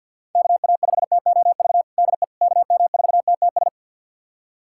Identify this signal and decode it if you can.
Morse code, heard as a keyed tone, about 49 wpm, 700 Hz. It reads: KA5TOV DE KM4TTS
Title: USA Callsign DE Callsign - 6 Characters - 50wpm